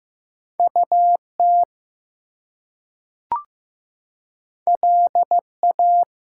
Morse code, heard as a keyed tone, about 15 wpm, 700 Hz. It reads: UT LA
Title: US States and Canadian Provinces 15wpm